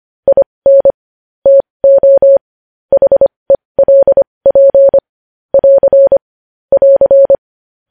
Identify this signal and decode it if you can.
Morse code, heard as a keyed tone, about 25 wpm, spaced as for 20 wpm, 550 Hz. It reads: IN TO HELP <AR> <AR>
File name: podcast / News headlines-1x-25x20